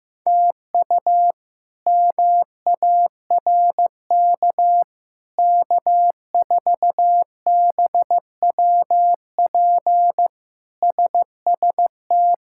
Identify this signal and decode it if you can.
Morse code, heard as a keyed tone, about 16 wpm, 700 Hz. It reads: TU MARK K4BWP SST
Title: K1USN SST QSO Practice - 15wpm